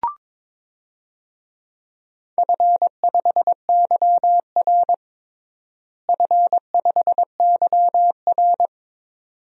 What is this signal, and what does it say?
Morse code, heard as a keyed tone, about 23 wpm, 700 Hz. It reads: F5YR F5YR
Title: DXCC-100 Call Signs - 4 Characters - Repeated 2x - 22wpm